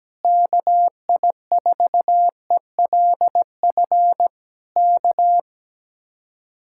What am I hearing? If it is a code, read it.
Morse code, heard as a keyed tone, about 17 wpm, 700 Hz. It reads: KI4ELF K
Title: Speed-Racing Common QSO Phrases Encore - 17wpm